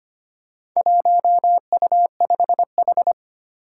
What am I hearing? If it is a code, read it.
Morse code, heard as a keyed tone, about 25 wpm, 700 Hz. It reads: 1U5H
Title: Four Challenging Letter-Number Combinations 25wpm